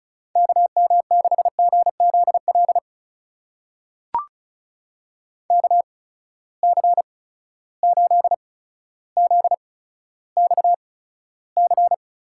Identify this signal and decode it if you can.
Morse code, heard as a keyed tone, about 36 wpm, spaced as for 15 wpm, 700 Hz. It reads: KM6GZL KC8ZXC
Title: USA Call Signs - 6 Characters - 8x Character Spacing - 35wpm